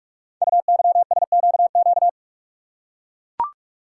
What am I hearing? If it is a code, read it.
Morse code, heard as a keyed tone, about 44 wpm, 700 Hz. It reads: UYSQQ
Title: Code Groups - 5 Letters - 45wpm